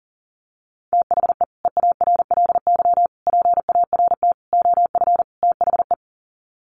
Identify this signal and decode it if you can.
Morse code, heard as a keyed tone, about 40 wpm, 700 Hz. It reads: THE EARLY PART OF THE
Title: Sets of 5 Words Encore2 40wpm